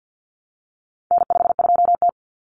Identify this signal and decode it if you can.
Morse code, heard as a keyed tone, about 50 wpm, 700 Hz. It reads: N52T
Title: Common Cut Numbers - 4-Digit 50wpm